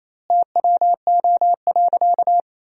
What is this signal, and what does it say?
Morse code, heard as a keyed tone, about 28 wpm, 700 Hz. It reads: TWO.
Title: 2x Word-Spacing - Sentences from Top 200 Words - 28wpm